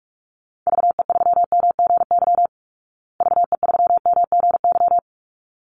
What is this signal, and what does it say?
Morse code, heard as a keyed tone, about 44 wpm, 700 Hz. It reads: VE3MGY VE3MGY
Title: Call Signs Repeated 3x - 45wpm